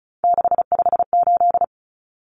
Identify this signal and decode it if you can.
Morse code, heard as a keyed tone, about 34 wpm, 700 Hz. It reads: 658
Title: Numbers - 3-Digit 35wpm